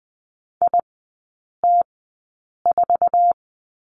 Morse code, heard as a keyed tone, about 20 wpm, spaced as for 8 wpm, 700 Hz. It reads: IT4
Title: 053 Lesson 10e - Letter r - Three Characters 20wpm